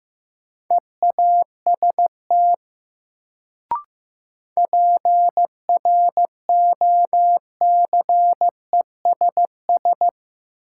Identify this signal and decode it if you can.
Morse code, heard as a keyed tone, about 15 wpm, 700 Hz. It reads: EAST PROCESS
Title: Sets of 3 Words - New Word Drill - 15wpm